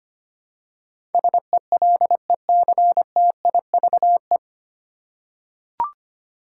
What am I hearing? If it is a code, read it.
Morse code, heard as a keyed tone, about 25 wpm, 700 Hz. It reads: SELECTIVE
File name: Nine Letter Words - 25wpm